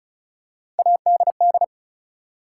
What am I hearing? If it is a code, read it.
Morse code, heard as a keyed tone, about 36 wpm, 700 Hz. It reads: ADD
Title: Three Letter Words - 35wpm